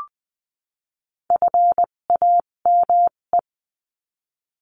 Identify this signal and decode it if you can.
Morse code, heard as a keyed tone, about 20 wpm, spaced as for 17 wpm, 700 Hz. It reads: FAME
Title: 257 Lesson 42b - Words 14-20wpm